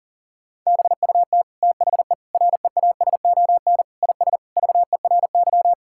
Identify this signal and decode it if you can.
Morse code, heard as a keyed tone, about 40 wpm, 700 Hz. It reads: BUT THE REASON IS VERY
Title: Sentences from Top 1600 Words 40wpm